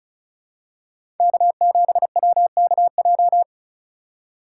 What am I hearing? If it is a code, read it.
Morse code, heard as a keyed tone, about 36 wpm, 700 Hz. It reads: K7WKJ
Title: USA Call Signs - 2x Character Spacing - 35wpm